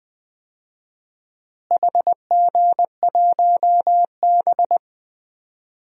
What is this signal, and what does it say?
Morse code, heard as a keyed tone, about 20 wpm, 700 Hz. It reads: HG1B
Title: DXCC-100 Call Signs - 4 Characters - Repeated 3x - 20wpm